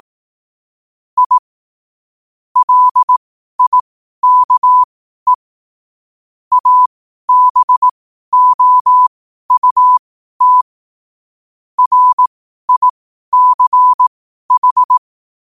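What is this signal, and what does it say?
Morse code, heard as a keyed tone, about 18 wpm, spaced as for 12 wpm, 1000 Hz. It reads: I LIKE ABOUT RICH
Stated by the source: 12 wpm Morse.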